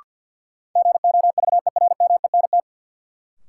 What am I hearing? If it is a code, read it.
Morse code, heard as a keyed tone, about 53 wpm, 700 Hz. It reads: GOVERMENT
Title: Sentences from Top 1300 Words - New Word Drill Alt - 50wpm